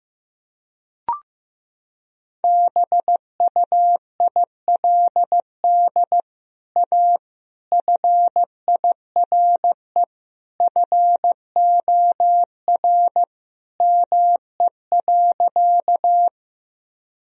Morse code, heard as a keyed tone, about 15 wpm, 700 Hz. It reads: BUILD A FIRE FOR ME.
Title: Sentences from Top 1000 Words - Review Part 1 - 15wpm